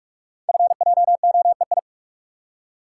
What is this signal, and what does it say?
Morse code, heard as a keyed tone, about 44 wpm, 700 Hz. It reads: FJOEI
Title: Code Groups - 5 Letters - 45wpm